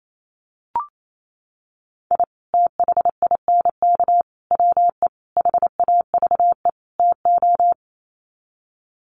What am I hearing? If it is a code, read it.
Morse code, heard as a keyed tone, about 28 wpm, 700 Hz. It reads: I THINK WE HAVE TO
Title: Sets of 5 Words 28wpm